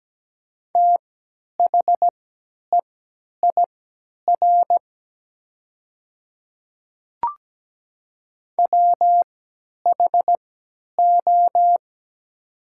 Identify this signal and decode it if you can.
Morse code, heard as a keyed tone, about 17 wpm, spaced as for 10 wpm, 700 Hz. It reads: THEIR WHO
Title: 3x Character Spacing - Top 100 Words - 17wpm